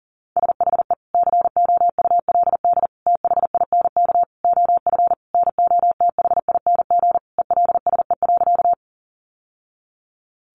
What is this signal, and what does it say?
Morse code, heard as a keyed tone, about 40 wpm, 700 Hz. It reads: SHE COULD THINK OF NOTHING ELSE.
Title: Sentences from Top 800 Words 40wpm